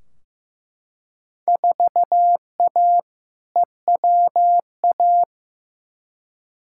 Morse code, heard as a keyed tone, about 15 wpm, 700 Hz. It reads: 4A EWA
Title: Field Day Exchanges - 15wpm